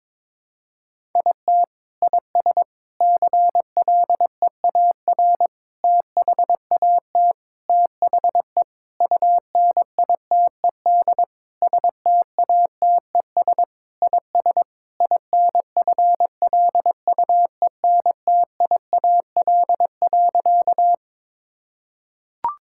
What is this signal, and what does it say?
Morse code, heard as a keyed tone, about 23 wpm, 700 Hz. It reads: IT IS CLEAR THAT THE UNITED STATES IS INFLUENTIAL.
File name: practice / 22wpm / Sentences with 4-Word Phrases - 22wpm